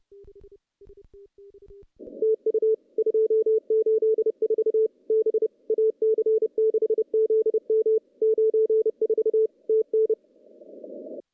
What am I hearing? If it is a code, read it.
Morse code, heard as a keyed tone, about 30 wpm, 440 Hz. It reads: TU 284 B AC6ZM 94 TN
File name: contests / CW-SS-k5tr- -0400